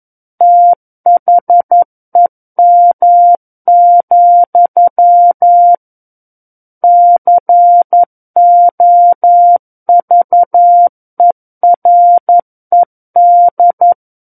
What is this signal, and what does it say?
Morse code, heard as a keyed tone, about 11 wpm, 700 Hz. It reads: THEM, COVERED
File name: 11 WPM